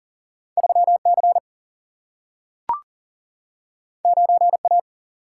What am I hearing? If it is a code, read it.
Morse code, heard as a keyed tone, about 40 wpm, 700 Hz. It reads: 3C 9A
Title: Single-Letter Single-Number 40wpm